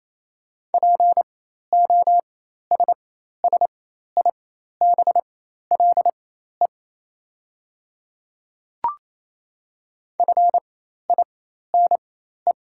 Morse code, heard as a keyed tone, about 27 wpm, spaced as for 13 wpm, 700 Hz. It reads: POSSIBLE FINE
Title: 4x Character Spacing - Top 500 Words - 28wpm